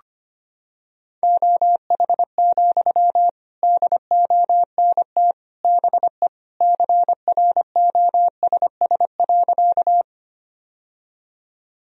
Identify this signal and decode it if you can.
Morse code, heard as a keyed tone, about 25 wpm, 700 Hz. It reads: OH, DONT BE CROSS.
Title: Sentences from Top 1000 Words - Review Part 1 - 25wpm